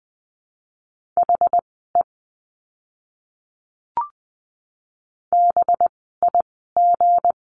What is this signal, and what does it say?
Morse code, heard as a keyed tone, about 20 wpm, spaced as for 14 wpm, 700 Hz. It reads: HE BIG
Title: 192 Lesson 31f - Letter v - Words 20wpm